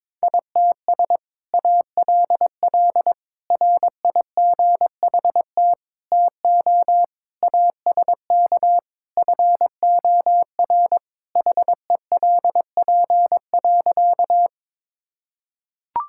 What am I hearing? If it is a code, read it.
Morse code, heard as a keyed tone, about 22 wpm, 700 Hz. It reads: ITS ALL RIGHT TO ASK FOR HELP.
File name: practice / 22wpm / Sentences with 2-Word Phrases - 22wpm